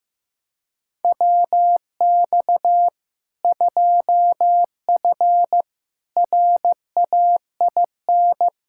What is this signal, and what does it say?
Morse code, heard as a keyed tone, about 15 wpm, 700 Hz. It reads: WX 2F RAIN
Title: K1USN SST QSO Practice 15wpm